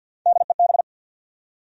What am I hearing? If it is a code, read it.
Morse code, heard as a keyed tone, about 52 wpm, 700 Hz. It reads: DEB
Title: USA Common Names - Max 3 Letters - 50wpm